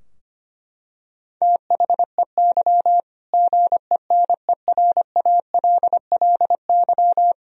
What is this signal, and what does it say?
Morse code, heard as a keyed tone, about 25 wpm, 700 Hz. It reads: THEY GENERALLY
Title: Sentences from Top 1000 Words - Review Part 4 - 25wpm